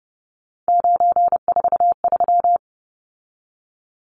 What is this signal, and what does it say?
Morse code, heard as a keyed tone, about 30 wpm, 700 Hz. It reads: 943
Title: Numbers - 3-Digit 30wpm